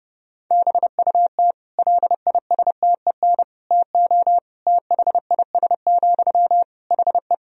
Morse code, heard as a keyed tone, about 28 wpm, 700 Hz. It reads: BUT LISTEN TO THIS, HE
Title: Sentences from Top 400 Words and Abbreviations 30wpm